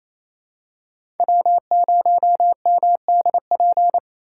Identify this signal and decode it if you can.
Morse code, heard as a keyed tone, about 28 wpm, 700 Hz. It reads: W0MDP
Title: USA Call Signs - 5 Characters - 4x Character Spacing - 28wpm